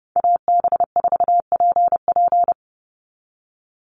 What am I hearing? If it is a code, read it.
Morse code, heard as a keyed tone, about 32 wpm, 700 Hz. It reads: AB4PP
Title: NA SOTA Chaser Callsigns 30wpm